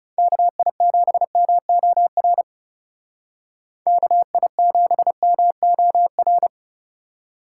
Morse code, heard as a keyed tone, about 37 wpm, 700 Hz. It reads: KI7MOR KI7MOR
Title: Speed Racing - USA 2x3 Call Signs 30wpm